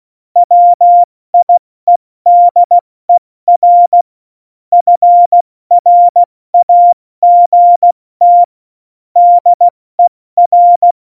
Morse code, heard as a keyed tone, about 16 wpm, 700 Hz. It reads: WIEDER FRAGT DER
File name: Morsekurs / Witze / Witz 03. November 2020, 70 F80